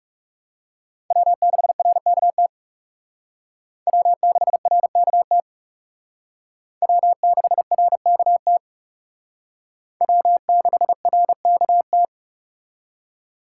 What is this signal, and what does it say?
Morse code, heard as a keyed tone, about 39 wpm, 700 Hz. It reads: W6RKT W6RKT W6RKT W6RKT